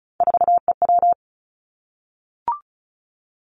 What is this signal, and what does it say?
Morse code, heard as a keyed tone, about 34 wpm, 700 Hz. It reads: 4EW
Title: Three Challenging Letter-Number Combinations 35wpm